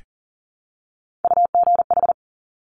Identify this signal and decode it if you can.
Morse code, heard as a keyed tone, about 41 wpm, 700 Hz. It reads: UGH
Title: Top Three Letter Combinations in Words 40wpm